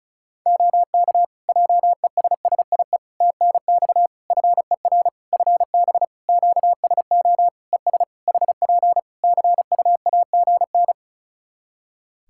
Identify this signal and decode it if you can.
Morse code, heard as a keyed tone, about 36 wpm, 700 Hz. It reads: OK JESSIE TNX FER FB QSO ES HP CUAGN
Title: Practice QSO - Exchange 4 - Ending - 35wpm